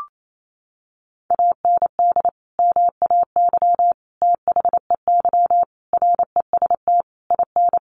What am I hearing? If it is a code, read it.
Morse code, heard as a keyed tone, about 28 wpm, 700 Hz. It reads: AND MAY THEY REST IN
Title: Sentences from Top 1500 Words 28wpm